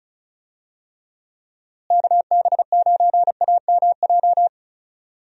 Morse code, heard as a keyed tone, about 34 wpm, 700 Hz. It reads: KB9AMJ
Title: USA Call Signs - 6 Characters - Repeated 3x - 35wpm